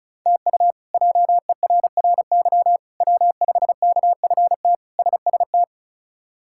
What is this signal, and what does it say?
Morse code, heard as a keyed tone, about 36 wpm, 700 Hz. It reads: TU JERRY W5KFT SST
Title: K1USN SST QSO Practice - 35wpm